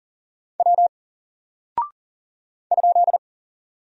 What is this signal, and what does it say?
Morse code, heard as a keyed tone, about 40 wpm, 700 Hz. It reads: W ?
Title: Single Character (Letter:Number:Punctuation:Prosign) 40wpm